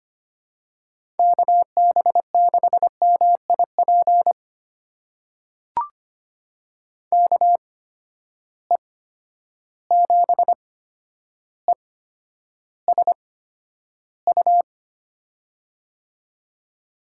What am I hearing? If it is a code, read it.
Morse code, heard as a keyed tone, about 25 wpm, spaced as for 11 wpm, 700 Hz. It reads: KB6MIP KE7ESU
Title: USA Call Signs - 6 Characters - 8x Character Spacing - 25wpm